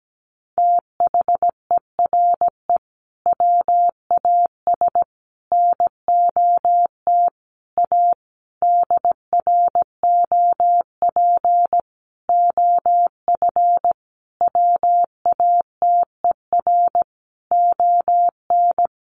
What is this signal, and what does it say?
Morse code, heard as a keyed tone, about 17 wpm, 700 Hz. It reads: THERE WAS NOT A DROP OF WATER ON
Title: Sentences from Top 1400 Words 17wpm